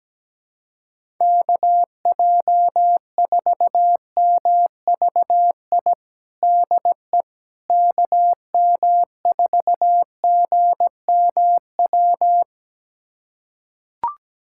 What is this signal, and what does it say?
Morse code, heard as a keyed tone, about 17 wpm, 700 Hz. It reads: KJ4MVI DE KM4GMW
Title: USA Callsign DE Callsign - 6 Characters - 17wpm